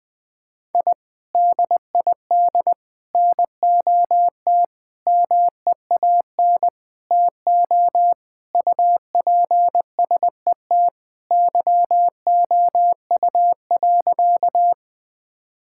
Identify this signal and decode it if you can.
Morse code, heard as a keyed tone, about 21 wpm, 700 Hz. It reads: I DID NOT MEAN TO UPSET YOU.
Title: Sentences with 5-Word Phrases - 20wpm